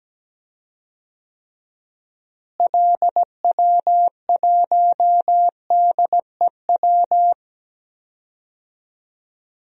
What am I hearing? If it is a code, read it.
Morse code, heard as a keyed tone, about 17 wpm, 700 Hz. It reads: LW1DEW
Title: DXCC-100 Call Signs - 6 Characters - Repeated 2x - 17wpm